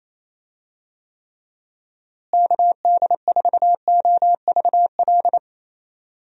Morse code, heard as a keyed tone, about 28 wpm, 700 Hz. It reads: KD4OVL
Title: USA Call Signs - 6 Characters - Repeated 3x - 28wpm